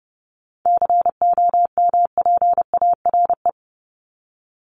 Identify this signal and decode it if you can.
Morse code, heard as a keyed tone, about 30 wpm, 700 Hz. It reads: COMPARE
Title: ED Suffix - Top 100 Words with Base Words 30wpm